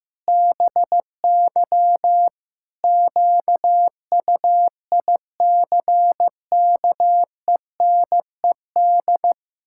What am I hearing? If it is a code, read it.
Morse code, heard as a keyed tone, about 15 wpm, 700 Hz. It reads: BY QUICKENED
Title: Pangram Challenge v2 - 15wpm